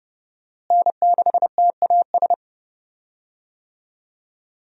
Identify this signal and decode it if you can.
Morse code, heard as a keyed tone, about 29 wpm, 700 Hz. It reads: N6TAS
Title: USA Call Signs - 5 Characters - Repeated 2x - 30wpm